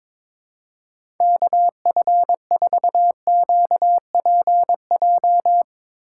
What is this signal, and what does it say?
Morse code, heard as a keyed tone, about 21 wpm, 700 Hz. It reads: KF4QPJ
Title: USA Call Signs - 6 Characters - 8x Character Spacing - 22wpm